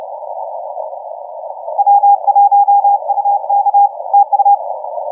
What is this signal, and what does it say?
Morse code, heard as a keyed tone, about 32 wpm, 810 Hz. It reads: W1UK TU
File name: XZ1Z 1425UTC 15m